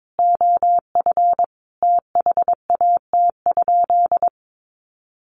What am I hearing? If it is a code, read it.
Morse code, heard as a keyed tone, about 22 wpm, 700 Hz. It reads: OF THAT?
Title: Sentences from Top 200 Words 22wpm